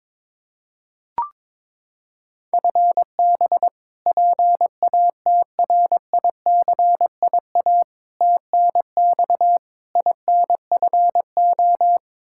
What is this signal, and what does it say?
Morse code, heard as a keyed tone, about 21 wpm, 700 Hz. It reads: FB PATRICIA TNX INFO
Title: Practice QSO - Exchange 3 - More Info - 22wpm